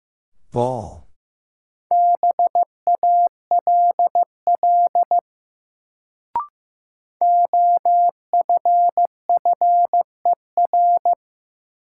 Morse code, heard as a keyed tone, about 15 wpm, 700 Hz. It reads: BALL OFFER
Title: Sentences from Top 900 Words - New Word Drill - 15wpm